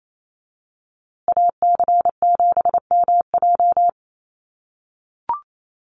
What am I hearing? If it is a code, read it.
Morse code, heard as a keyed tone, about 28 wpm, 700 Hz. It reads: AC7MJ
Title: USA Call Signs Repeated 2x 28wpm